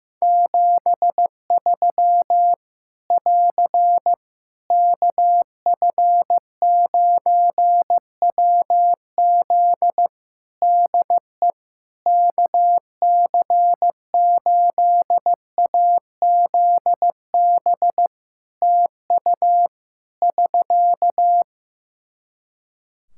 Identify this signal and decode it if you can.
Morse code, heard as a keyed tone, about 15 wpm, 700 Hz. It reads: 73 <AR> KF9WZ DE KC8AZB TU <SK>
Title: Practice QSO - Exchange 4 - Ending - 15wpm